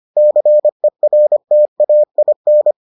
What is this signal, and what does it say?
Morse code, heard as a keyed tone, about 25 wpm, 600 Hz. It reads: CERTAIN